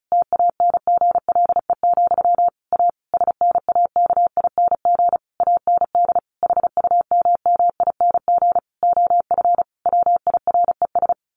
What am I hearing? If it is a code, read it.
Morse code, heard as a keyed tone, about 34 wpm, 700 Hz. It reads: TANGLE, A SNAKING AND HUMMING OF WIRES
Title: A Sound Of Thunder 35wpm